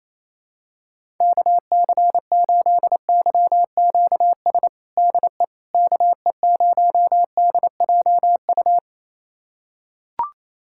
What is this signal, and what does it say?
Morse code, heard as a keyed tone, about 28 wpm, 700 Hz. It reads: KC8YQS DE KE0DJU
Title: USA Callsign DE Callsign 28wpm